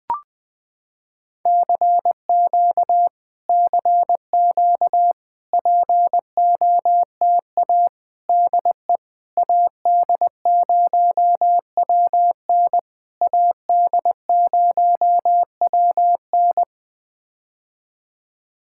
Morse code, heard as a keyed tone, about 20 wpm, 700 Hz. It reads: CQ CQ POTA DE AD0WN AD0WN
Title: POTA QSO Practice - 20wpm